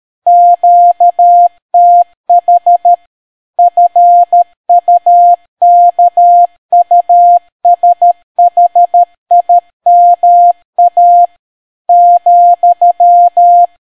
Morse code, heard as a keyed tone, about 13 wpm, 700 Hz. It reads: QTH FUKUSHIMA ,